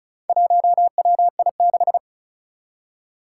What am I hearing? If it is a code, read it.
Morse code, heard as a keyed tone, about 36 wpm, 700 Hz. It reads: 1WI6
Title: Four Challenging Letter-Number Combinations 35wpm